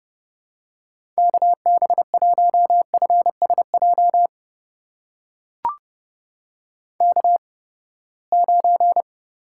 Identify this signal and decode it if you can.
Morse code, heard as a keyed tone, about 30 wpm, spaced as for 21 wpm, 700 Hz. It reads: KB1FSJ K9
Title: USA Call Signs - 8x Character Spacing - 30wpm